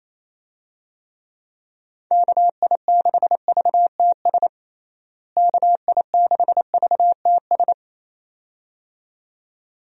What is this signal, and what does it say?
Morse code, heard as a keyed tone, about 27 wpm, 700 Hz. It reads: KI6VTS KI6VTS
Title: USA Call Signs - 6 Characters - Repeated 2x - 28wpm